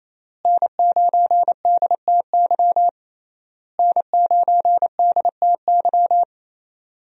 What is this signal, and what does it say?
Morse code, heard as a keyed tone, about 28 wpm, 700 Hz. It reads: N9DTY N9DTY
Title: USA Call Signs - 5 Characters - Repeated 3x - 28wpm